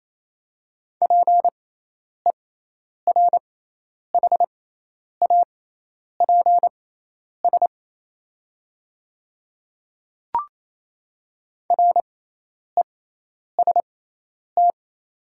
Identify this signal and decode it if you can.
Morse code, heard as a keyed tone, about 28 wpm, spaced as for 10 wpm, 700 Hz. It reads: PERHAPS REST
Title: 6x Character Spacing - Top 500 Words - 28wpm